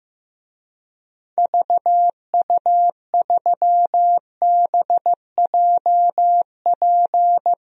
Morse code, heard as a keyed tone, about 15 wpm, 700 Hz. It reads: VU3BJP
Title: DXCC-100 Call Signs - 6 Characters - 15wpm